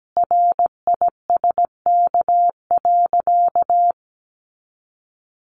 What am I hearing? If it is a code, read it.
Morse code, heard as a keyed tone, about 17 wpm, 700 Hz. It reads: RISK.
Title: Sentences from Top 600 Words 17wpm